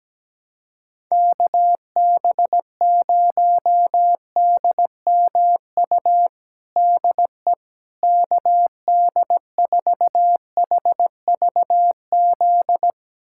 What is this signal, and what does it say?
Morse code, heard as a keyed tone, about 17 wpm, 700 Hz. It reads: KB0DMU DE KD4HVZ
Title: USA Callsign DE Callsign 17wpm